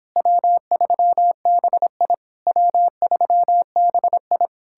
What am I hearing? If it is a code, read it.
Morse code, heard as a keyed tone, about 31 wpm, 700 Hz. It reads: W3BI W3BI